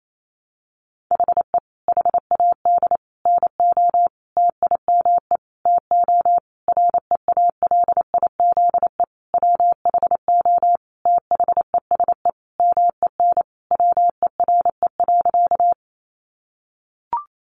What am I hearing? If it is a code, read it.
Morse code, heard as a keyed tone, about 28 wpm, 700 Hz. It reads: HE HAD NO TIME TO REALIZE WHO THESE MEN WERE.
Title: Sentences from Top 1000 Words 28wpm